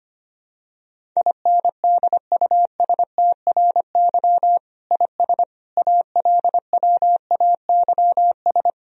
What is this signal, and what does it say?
Morse code, heard as a keyed tone, about 25 wpm, 700 Hz. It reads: INDUSTRY IS ALWAYS
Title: Sentences from Top 1000 Words - Review Part 4 - 25wpm